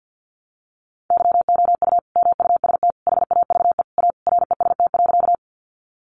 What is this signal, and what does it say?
Morse code, heard as a keyed tone, about 50 wpm, 700 Hz. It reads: YOU MUST HAVE A REST.
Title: Sentences from Top 500 Words 50wpm